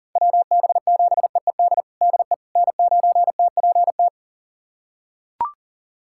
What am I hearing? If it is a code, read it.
Morse code, heard as a keyed tone, about 40 wpm, 700 Hz. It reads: WB7EED DE N9TPT
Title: USA Callsign DE Callsign 40wpm